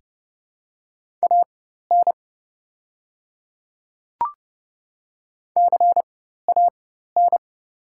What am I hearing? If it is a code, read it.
Morse code, heard as a keyed tone, about 30 wpm, spaced as for 14 wpm, 690 Hz. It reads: AN CAN
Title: 4x Character Spacing - Top 100 Words - 30wpm